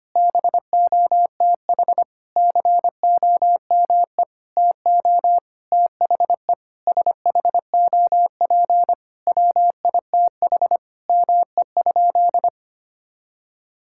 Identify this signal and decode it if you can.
Morse code, heard as a keyed tone, about 25 wpm, 700 Hz. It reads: BOTH COME TO THE SHOP WITH ME?
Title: Sentences from Top 1600 Words 25wpm